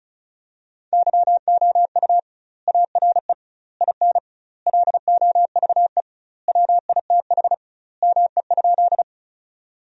Morse code, heard as a keyed tone, about 36 wpm, 700 Hz. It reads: YOU ARE IN LOVE WITH ME?
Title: 2x Word-Spacing - Sentences from Top 400 Words - 35wpm